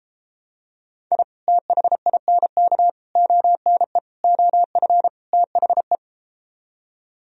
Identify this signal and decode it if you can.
Morse code, heard as a keyed tone, about 33 wpm, 700 Hz. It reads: I THINK ONE OF THE